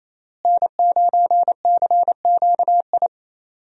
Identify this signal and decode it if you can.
Morse code, heard as a keyed tone, about 28 wpm, 700 Hz. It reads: N9CQI
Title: USA Call Signs - 5 Characters - Repeated 2x - 28wpm